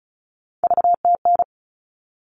Morse code, heard as a keyed tone, about 34 wpm, 700 Hz. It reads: VTN
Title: All Cut Numbers - 3-Digit 35wpm